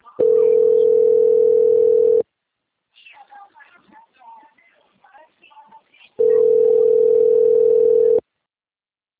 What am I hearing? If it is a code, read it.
Ringback tone (North America), 440 + 480 Hz.